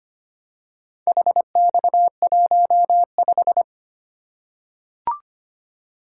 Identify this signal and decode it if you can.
Morse code, heard as a keyed tone, about 25 wpm, 700 Hz. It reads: HX15
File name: practice / 25wpm / Four Letter-Number Combinations - 25wpm